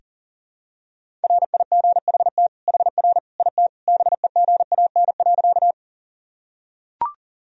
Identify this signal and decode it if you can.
Morse code, heard as a keyed tone, about 40 wpm, 700 Hz. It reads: RIGHT HR IT BEGAN.
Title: Sentences from Top 300 Words and Abbreviations 40wpm